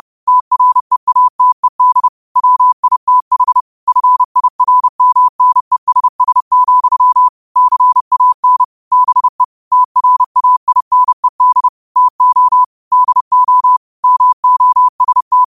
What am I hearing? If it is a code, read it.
Morse code, heard as a keyed tone, about 30 wpm, 1000 Hz. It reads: TREATED WITH FIRMNESS, CAN BE TRAINED TO DO MOST
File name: qotdfiles / 30wpm